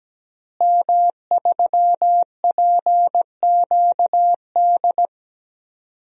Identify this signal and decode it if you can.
Morse code, heard as a keyed tone, about 17 wpm, 700 Hz. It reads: M3PQD
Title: DXCC-100 Call Signs - 5 Characters - Repeated 3x - 17wpm